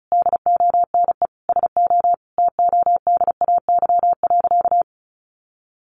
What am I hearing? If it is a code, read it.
Morse code, heard as a keyed tone, about 34 wpm, 700 Hz. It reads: DONE SO TODAY.
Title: Sentences from Top 1500 Words 35wpm